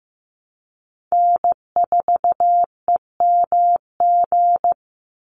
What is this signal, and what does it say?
Morse code, heard as a keyed tone, about 15 wpm, 700 Hz. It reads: N4EMG
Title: NA SOTA Chaser Callsigns 15wpm